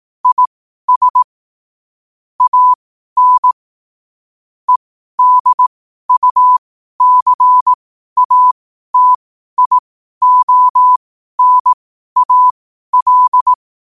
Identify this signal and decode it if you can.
Morse code, heard as a keyed tone, about 18 wpm, spaced as for 12 wpm, 1000 Hz. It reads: IS AN EDUCATIONAL
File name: qotdfiles / 12wpm